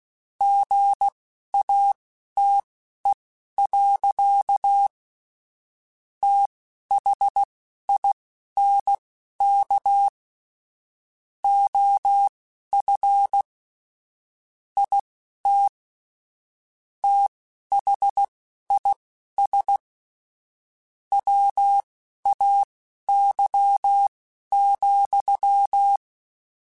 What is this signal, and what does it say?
Morse code, heard as a keyed tone, about 16 wpm, spaced as for 12 wpm, 800 Hz. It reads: GATE. THINK OF IT THIS WAY,
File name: Technical Subjects / Technical Subjects-mp3s / OR Gate @10